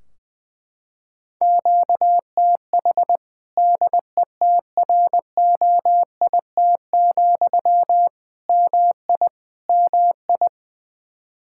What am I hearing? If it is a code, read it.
Morse code, heard as a keyed tone, about 20 wpm, 700 Hz. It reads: QTH DETROIT, MI MI
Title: Practice QSO - Exchange 1 - Beginning - 20wpm